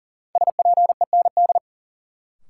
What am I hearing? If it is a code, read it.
Morse code, heard as a keyed tone, about 40 wpm, 700 Hz. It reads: SPEND
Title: Sentences from Top 800 Words - New Word Drill - 40wpm